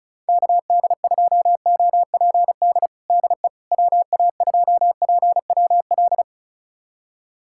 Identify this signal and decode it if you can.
Morse code, heard as a keyed tone, about 36 wpm, 700 Hz. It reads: KD2OPD DE WA2PWL
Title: USA Callsign DE Callsign 35wpm